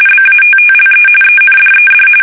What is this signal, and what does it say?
RTTY, about 45.45 baud. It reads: BATTLE ENDED
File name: fsk850 / usnavy3